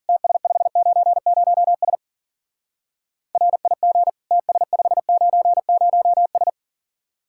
Morse code, heard as a keyed tone, about 49 wpm, 700 Hz. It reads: TS590S RIG TS590S
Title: Speed-Racing Common QSO Phrases - 40wpm